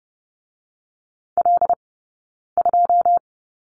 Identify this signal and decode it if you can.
Morse code, heard as a keyed tone, about 30 wpm, 700 Hz. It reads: L 2
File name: course / 30:10wpm / 086 Lesson 15d - Number 5 - Two Characters 30wpm